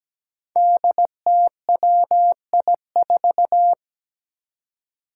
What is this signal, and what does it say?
Morse code, heard as a keyed tone, about 17 wpm, 700 Hz. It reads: DTWI4
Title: Five Challenging Letter-Number Combinations 17wpm